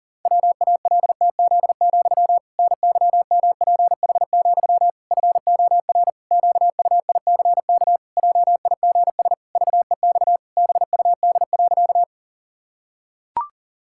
Morse code, heard as a keyed tone, about 40 wpm, 700 Hz. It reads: WALTZ, NYMPH, FOR QUICK JIGS VEX BUD.
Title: Pangram Challenge v2 - 40wpm